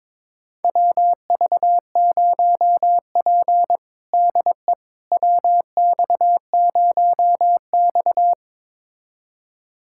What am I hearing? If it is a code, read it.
Morse code, heard as a keyed tone, about 22 wpm, 700 Hz. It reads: WV0P DE WX0X
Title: USA Callsign DE Callsign - 4 Characters - 22wpm